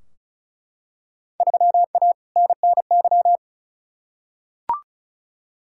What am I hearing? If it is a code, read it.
Morse code, heard as a keyed tone, about 36 wpm, 700 Hz. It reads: 3A NNY
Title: Field Day Exchanges - 35wpm